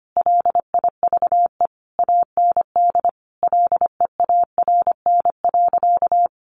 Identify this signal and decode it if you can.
Morse code, heard as a keyed tone, about 25 wpm, 700 Hz. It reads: LIVE AND LEARN.
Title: Sentences from Top 800 Words 25wpm